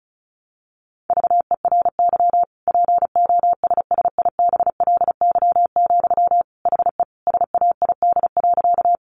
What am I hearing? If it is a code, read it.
Morse code, heard as a keyed tone, about 34 wpm, 700 Hz. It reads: VERY POSSIBLY, HE SAID.
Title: Sentences from Top 1300 Words 35wpm